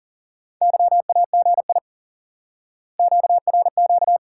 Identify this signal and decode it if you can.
Morse code, heard as a keyed tone, about 38 wpm, 700 Hz. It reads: YAGI QRQ
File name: IQE Course 3 - Lesson 89 - YAGI - Two Elements - 40wpm